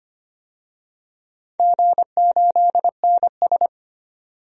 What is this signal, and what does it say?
Morse code, heard as a keyed tone, about 25 wpm, 690 Hz. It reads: G8NS
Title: DXCC-100 Call Signs - 4 Characters - 25wpm